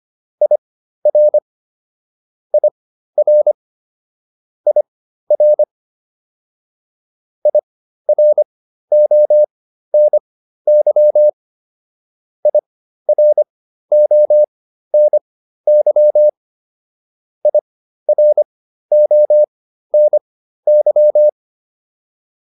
Morse code, heard as a keyed tone, about 25 wpm, spaced as for 13 wpm, 600 Hz. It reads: IR IR IR IRONY IRONY IRONY
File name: IR5 13